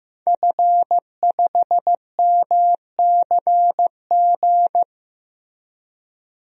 Morse code, heard as a keyed tone, about 16 wpm, 700 Hz. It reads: F5MCG
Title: DXCC-100 Call Signs - 5 Characters - 15wpm